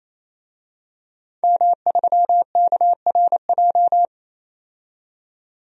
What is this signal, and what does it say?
Morse code, heard as a keyed tone, about 27 wpm, 700 Hz. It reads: M3KRJ
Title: DXCC-100 Call Signs - 5 Characters - 28wpm